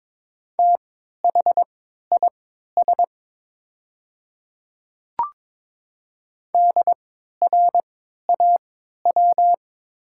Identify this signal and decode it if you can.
Morse code, heard as a keyed tone, about 22 wpm, spaced as for 12 wpm, 690 Hz. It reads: THIS DRAW
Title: 3x Character Spacing - Top 300 Words - 22wpm